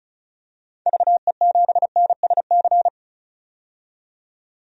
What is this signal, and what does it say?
Morse code, heard as a keyed tone, about 36 wpm, 700 Hz. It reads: VE7NSC
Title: DXCC-100 Call Signs - 6 Characters - 35wpm